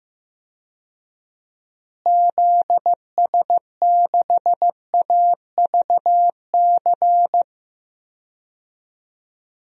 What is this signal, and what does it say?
Morse code, heard as a keyed tone, about 15 wpm, 700 Hz. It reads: ZS6AVC
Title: DXCC-100 Call Signs - 6 Characters - Repeated 3x - 15wpm